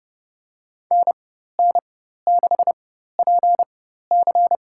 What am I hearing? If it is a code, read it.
Morse code, heard as a keyed tone, about 29 wpm, spaced as for 14 wpm, 700 Hz. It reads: NN6PC
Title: USA Call Signs - 5 Characters - 4x Character Spacing - 30wpm